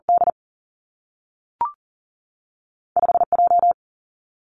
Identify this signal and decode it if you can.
Morse code, heard as a keyed tone, about 41 wpm, 700 Hz. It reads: D 5 J
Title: Two Challenging Letter-Number Combinations 40wpm